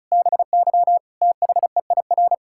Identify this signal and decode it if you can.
Morse code, heard as a keyed tone, about 36 wpm, 700 Hz. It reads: BY THEIR
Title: Pangram Challenge v2 - 35wpm